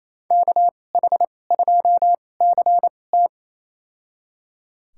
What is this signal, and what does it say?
Morse code, heard as a keyed tone, about 28 wpm, spaced as for 20 wpm, 700 Hz. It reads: KH2CT
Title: USA Call Signs - 5 Characters - 2x Character Spacing - 28wpm